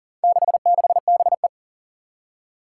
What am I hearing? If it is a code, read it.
Morse code, heard as a keyed tone, about 40 wpm, 700 Hz. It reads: 66BE
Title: Four Challenging Letter-Number Combinations 40wpm